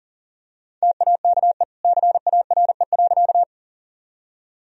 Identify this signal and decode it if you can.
Morse code, heard as a keyed tone, about 40 wpm, 700 Hz. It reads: TAKE CARE.
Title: Sentences from Top 1000 Words - Review Part 1 - 40wpm